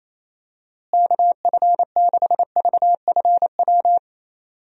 Morse code, heard as a keyed tone, about 28 wpm, 700 Hz. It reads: KF6VFW
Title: USA Call Signs - 6 Characters - Repeated 3x - 28wpm